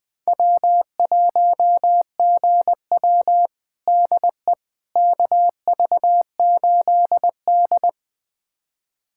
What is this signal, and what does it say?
Morse code, heard as a keyed tone, about 20 wpm, 700 Hz. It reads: W1GW DE KV8D
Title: USA Callsign DE Callsign - 4 Characters - 20wpm